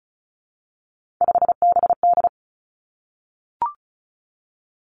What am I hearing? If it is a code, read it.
Morse code, heard as a keyed tone, about 36 wpm, 700 Hz. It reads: 5BD
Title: Three Challenging Letter-Number Combinations 35wpm